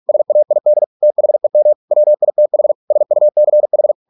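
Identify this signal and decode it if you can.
Morse code, heard as a keyed tone, about 48 wpm, 600 Hz. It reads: SAID THEM WITH SUCH
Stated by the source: Morse practice at 45 wpm, the novel Never by Ken Follett in English, from CW med Gnister.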